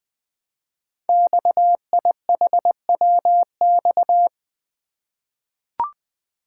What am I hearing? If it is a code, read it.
Morse code, heard as a keyed tone, about 20 wpm, 700 Hz. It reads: XIHWX
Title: Code Groups - 5 Letters - 20wpm